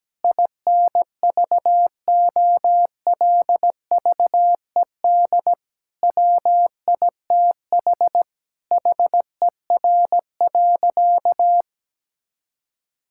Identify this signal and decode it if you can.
Morse code, heard as a keyed tone, about 17 wpm, 700 Hz. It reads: INVOLVED WITH HER.
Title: Sentences from Top 1000 Words - Review Part 5 - 17wpm